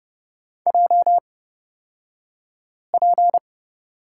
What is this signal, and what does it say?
Morse code, heard as a keyed tone, about 30 wpm, spaced as for 5 wpm, 700 Hz. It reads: JP
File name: Lesson 36d - BT - Two Characters 30wpm